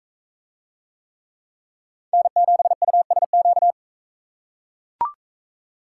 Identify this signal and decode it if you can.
Morse code, heard as a keyed tone, about 39 wpm, 700 Hz. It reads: N7USQ